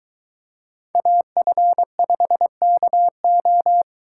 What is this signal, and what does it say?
Morse code, heard as a keyed tone, about 23 wpm, 700 Hz. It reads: AF5KO